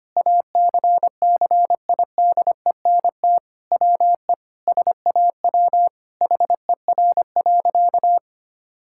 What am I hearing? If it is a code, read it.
Morse code, heard as a keyed tone, about 25 wpm, 700 Hz. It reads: ACCIDENT WE SAW HER.
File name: practice / 25wpm / Sentences with 4-Word Phrases - 25wpm